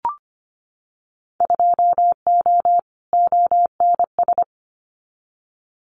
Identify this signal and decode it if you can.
Morse code, heard as a keyed tone, about 25 wpm, 700 Hz. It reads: 2O ONS
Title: Winter Field Day Exchanges Encore 25wpm